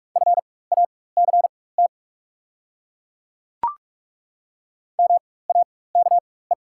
Morse code, heard as a keyed tone, about 42 wpm, spaced as for 20 wpm, 700 Hz. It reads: FACT MAKE
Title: 4x Character Spacing - Top 500 Words - 45wpm